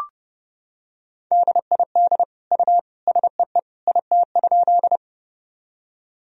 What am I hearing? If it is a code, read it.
Morse code, heard as a keyed tone, about 31 wpm, 700 Hz. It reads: DID U SEE IT?
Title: Sentences from Top 100 Words and Abbreviations 30wpm